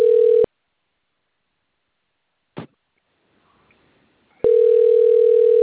Ringback tone (North America), 440 + 480 Hz.
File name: rg-1600-unknown- - - .40025